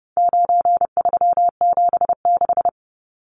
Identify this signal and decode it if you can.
Morse code, heard as a keyed tone, about 30 wpm, 700 Hz. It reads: 9376
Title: Numbers - 4-Digit 30wpm